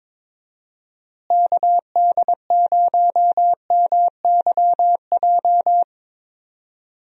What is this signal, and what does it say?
Morse code, heard as a keyed tone, about 22 wpm, 700 Hz. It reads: KD0MYJ
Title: USA Call Signs - 6 Characters - Repeated 2x - 22wpm